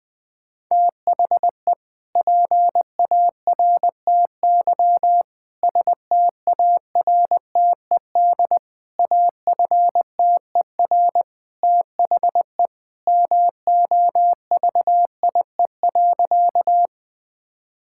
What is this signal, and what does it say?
Morse code, heard as a keyed tone, about 21 wpm, 700 Hz. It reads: THE PARTY STARTED AFTER THE MOVIE.
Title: Sentences with 2-Word Phrases - 20wpm